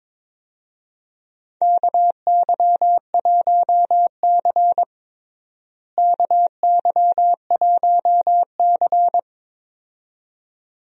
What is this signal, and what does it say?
Morse code, heard as a keyed tone, about 22 wpm, 700 Hz. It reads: KY1C KY1C
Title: USA Call Signs - 4 Characters - Repeated 2x - 22wpm